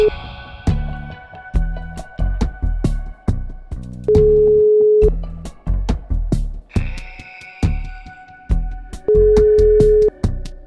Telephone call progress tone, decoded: ringback tone

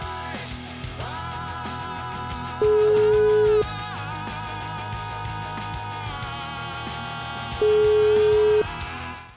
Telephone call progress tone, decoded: ringback tone

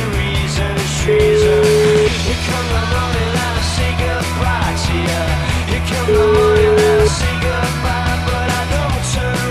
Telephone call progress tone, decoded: ringback tone

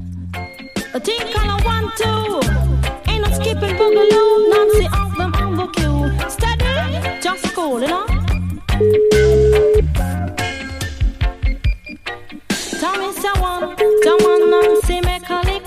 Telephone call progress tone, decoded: ringback tone

